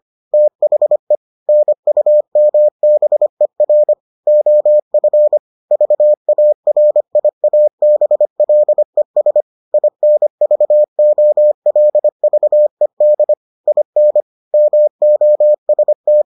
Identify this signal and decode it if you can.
Morse code, heard as a keyed tone, about 25 wpm, 600 Hz. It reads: THE NUMBER OF VARIABLES INVOLVED IN MOST